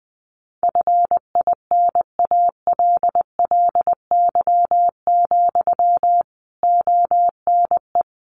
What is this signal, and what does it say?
Morse code, heard as a keyed tone, about 20 wpm, 700 Hz. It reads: FINALLY, ONE
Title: Sentences from Top 1300 Words 20wpm